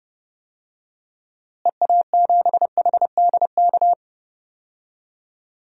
Morse code, heard as a keyed tone, about 31 wpm, 700 Hz. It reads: EA7HDK
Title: DXCC-100 Call Signs - 6 Characters - Repeated 3x - 30wpm